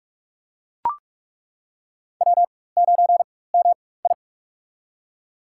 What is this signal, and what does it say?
Morse code, heard as a keyed tone, about 46 wpm, spaced as for 21 wpm, 700 Hz. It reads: W9MI
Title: USA Call Signs - 4 Characters - 4x Character Spacing - 45wpm